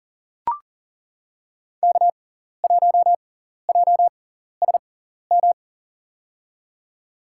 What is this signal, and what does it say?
Morse code, heard as a keyed tone, about 40 wpm, spaced as for 14 wpm, 700 Hz. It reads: K1JSM
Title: USA Call Signs - 5 Characters - 6x Character Spacing - 40wpm